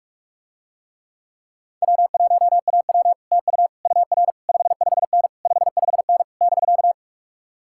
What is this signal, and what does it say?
Morse code, heard as a keyed tone, about 49 wpm, 700 Hz. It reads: W1AW TU UR 55N 55N <BK>